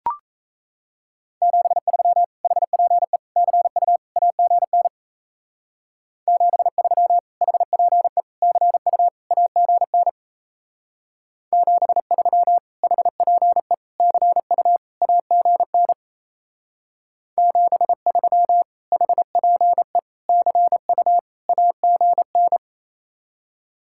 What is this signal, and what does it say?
Morse code, heard as a keyed tone, about 32 wpm, 700 Hz. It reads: 73 HPE CU AGN 73 HPE CU AGN 73 HPE CU AGN 73 HPE CU AGN
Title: Speed-Racing Common QSO Phrases - 28wpm